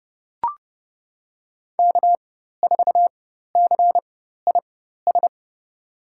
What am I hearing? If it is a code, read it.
Morse code, heard as a keyed tone, about 31 wpm, spaced as for 14 wpm, 700 Hz. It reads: K4CIS
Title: USA Call Signs - 5 Characters - 4x Character Spacing - 30wpm